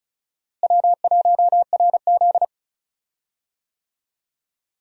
Morse code, heard as a keyed tone, about 36 wpm, 700 Hz. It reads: W1RZ
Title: USA Call Signs - 4 Characters - Repeated 2x - 35wpm